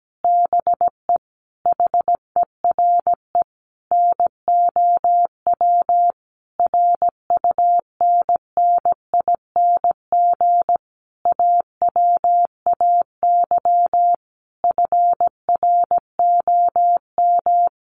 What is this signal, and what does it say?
Morse code, heard as a keyed tone, about 17 wpm, 700 Hz. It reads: BE HERE NOW RUNNING AWAY FROM
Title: A Sound Of Thunder 17wpm